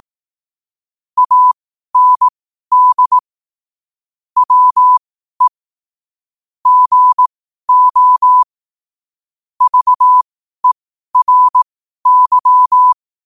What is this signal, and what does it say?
Morse code, heard as a keyed tone, about 18 wpm, spaced as for 12 wpm, 1000 Hz. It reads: AND WE GO VERY
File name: qotdfiles / 12wpm